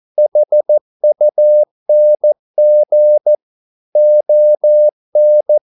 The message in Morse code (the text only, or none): HUNG ON